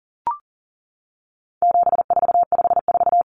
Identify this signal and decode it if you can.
Morse code, heard as a keyed tone, about 40 wpm, 700 Hz. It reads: 7454
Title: Numbers - 4-Digit 40wpm